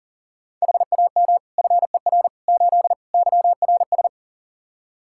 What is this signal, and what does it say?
Morse code, heard as a keyed tone, about 40 wpm, 700 Hz. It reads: HAM FER 8 YRS
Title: Practice QSO - Exchange 3 - More Info - 40wpm